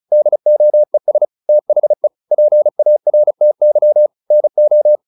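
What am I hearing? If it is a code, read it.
Morse code, heard as a keyed tone, about 35 wpm, 600 Hz. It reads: DOES THE PARTY NO
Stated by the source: Morse practice at 35 wpm from CW med Gnister, sending Ken Follett's novel Never in English.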